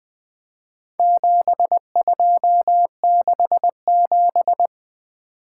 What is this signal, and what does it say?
Morse code, heard as a keyed tone, about 23 wpm, 700 Hz. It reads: 7267
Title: Speed-Racing - Four Letter-Number Combinations - 20wpm